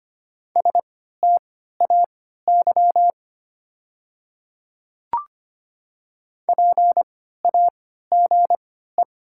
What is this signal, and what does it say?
Morse code, heard as a keyed tone, about 25 wpm, spaced as for 14 wpm, 700 Hz. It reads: STAY PAGE
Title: 3x Character Spacing - Top 500 Words - 25wpm